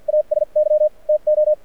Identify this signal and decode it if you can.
Morse code, heard as a keyed tone, about 53 wpm, spaced as for 44 wpm, 590 Hz. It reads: AS Y TO
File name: archived / CRSnet